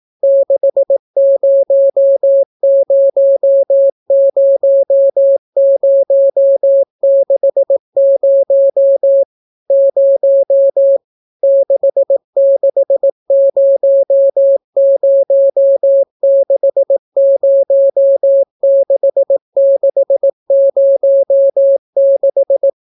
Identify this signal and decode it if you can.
Morse code, heard as a keyed tone, about 18 wpm, 550 Hz. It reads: 6000060 0 6600606606